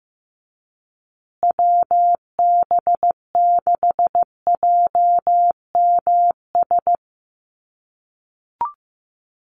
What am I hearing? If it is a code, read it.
Morse code, heard as a keyed tone, about 15 wpm, 700 Hz. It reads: WB6JMS
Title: USA Call Signs Repeated 2x 15wpm